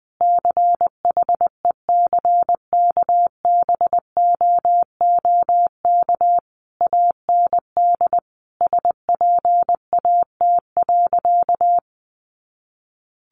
Morse code, heard as a keyed tone, about 19 wpm, 700 Hz. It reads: CHECKBOOK AND SPAT.
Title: A Sound Of Thunder 20wpm